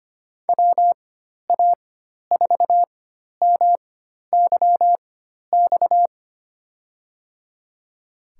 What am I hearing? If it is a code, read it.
Morse code, heard as a keyed tone, about 25 wpm, spaced as for 12 wpm, 700 Hz. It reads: WA4MYX
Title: USA Call Signs - 6 Characters - 4x Character Spacing - 25wpm